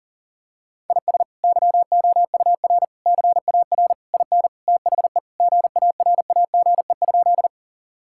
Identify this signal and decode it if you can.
Morse code, heard as a keyed tone, about 37 wpm, 700 Hz. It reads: IS YOUR CAR IN THE GARAGE?
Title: Sentences from Top 1600 Words 40wpm